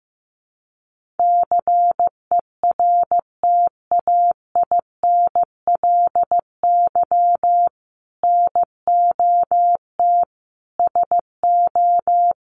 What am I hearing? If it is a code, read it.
Morse code, heard as a keyed tone, about 15 wpm, 700 Hz. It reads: CERTAINLY NOT SO
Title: Sentences from Top 1500 Words 15wpm